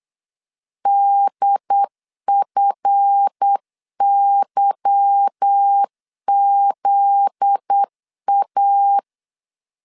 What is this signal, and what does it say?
Morse code, heard as a keyed tone, about 8 wpm, 800 Hz. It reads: DFYZA
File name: LEZIONI / 8 YJ45 / 8 Y 8wpm 3245